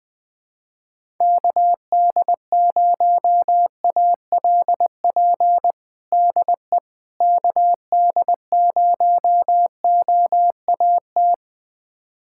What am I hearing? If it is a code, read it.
Morse code, heard as a keyed tone, about 20 wpm, 700 Hz. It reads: KD0ALP DE KD0OAT
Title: USA Callsign DE Callsign 20wpm